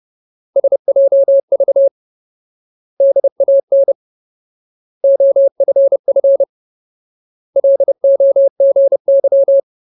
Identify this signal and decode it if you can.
Morse code, heard as a keyed tone, about 30 wpm, 550 Hz. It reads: SJV DAN OFF LOGY
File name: Ultimate Word List / 30WPM Ultimate